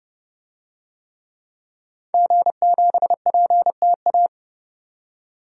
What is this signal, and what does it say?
Morse code, heard as a keyed tone, about 30 wpm, 700 Hz. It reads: G7PTA
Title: DXCC-100 Call Signs - 5 Characters - 30wpm